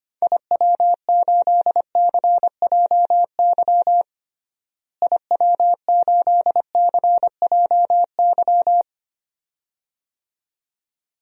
Morse code, heard as a keyed tone, about 25 wpm, 700 Hz. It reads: IW8CJY IW8CJY
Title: DXCC-100 Call Signs - 6 Characters - Repeated 2x - 25wpm